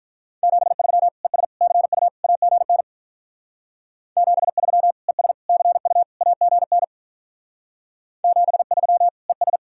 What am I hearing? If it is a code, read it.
Morse code, heard as a keyed tone, about 43 wpm, 700 Hz. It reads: 73 ES CU AGN 73 ES CU AGN 73 ES
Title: Speed-Racing Common QSO Phrases - 35wpm